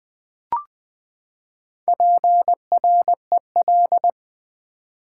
Morse code, heard as a keyed tone, about 20 wpm, 700 Hz. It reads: PREL
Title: Build A Word - 4 Letters At A Time - 11-Letter Words - 20wpm